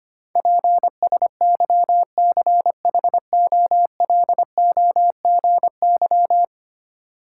Morse code, heard as a keyed tone, about 25 wpm, 690 Hz. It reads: PSYCHOLOGY
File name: 10 Letter Words - First 3 Letters Given - 25wpm